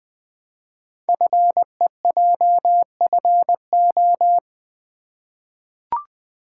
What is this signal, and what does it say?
Morse code, heard as a keyed tone, about 20 wpm, 700 Hz. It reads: FEJFO
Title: Code Groups - 5 Letters - 20wpm